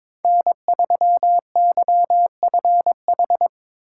Morse code, heard as a keyed tone, about 22 wpm, 700 Hz. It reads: N3YFH
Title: USA Call Signs - 5 Characters - 8x Character Spacing - 22wpm